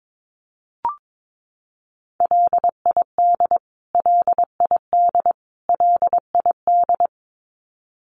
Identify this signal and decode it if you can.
Morse code, heard as a keyed tone, about 22 wpm, 700 Hz. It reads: LID LID LID
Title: Abbreviations Repeated 3x - 22wpm